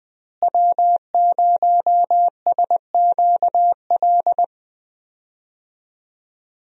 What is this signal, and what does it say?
Morse code, heard as a keyed tone, about 20 wpm, 700 Hz. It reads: W0SQL
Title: USA Call Signs - 5 Characters - Repeated 2x - 20wpm